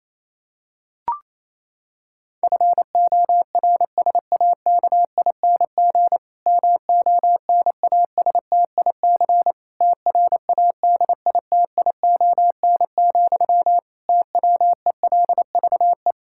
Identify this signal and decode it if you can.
Morse code, heard as a keyed tone, about 28 wpm, 700 Hz. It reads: FORSAKING MONASTIC TRADITION, TWELVE
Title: Pangram Challenge v2 - 28wpm